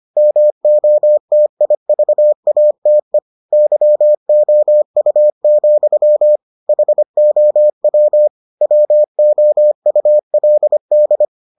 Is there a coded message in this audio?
Yes: Morse code from 0.2 to 11.2 s, heard as a keyed tone, about 25 wpm, 600 Hz. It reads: MOTIVATE YOU, HOW WOULD